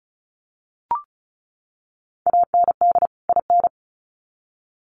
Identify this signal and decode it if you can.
Morse code, heard as a keyed tone, about 34 wpm, 700 Hz. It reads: AND IN
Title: Sets of 2 Words 35wpm